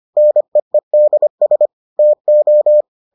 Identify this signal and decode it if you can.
Morse code, heard as a keyed tone, about 25 wpm, 600 Hz. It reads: NEEDS TO